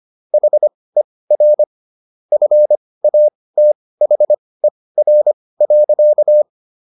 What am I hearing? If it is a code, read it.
Morse code, heard as a keyed tone, about 25 wpm, spaced as for 18 wpm, 600 Hz. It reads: HER FATHER.